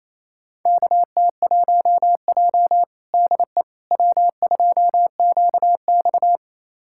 Morse code, heard as a keyed tone, about 28 wpm, 700 Hz. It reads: KT1J DE W2QX
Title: USA Callsign DE Callsign - 4 Characters - 28wpm